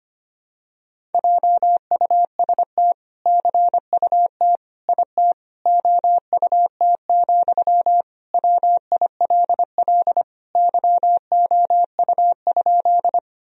Morse code, heard as a keyed tone, about 25 wpm, 700 Hz. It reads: JUST CUT IT OUT, WILL YOU?
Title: Sentences from Top 1000 Words - Review Part 1 - 25wpm